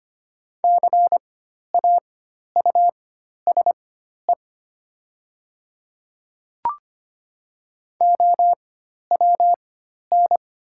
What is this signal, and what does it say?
Morse code, heard as a keyed tone, about 25 wpm, spaced as for 12 wpm, 700 Hz. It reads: CAUSE OWN
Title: 4x Character Spacing - Top 300 Words - 25wpm